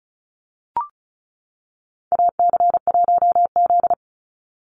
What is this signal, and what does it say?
Morse code, heard as a keyed tone, about 34 wpm, 700 Hz. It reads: AC1Z
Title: NA SOTA Activator Callsigns 35wpm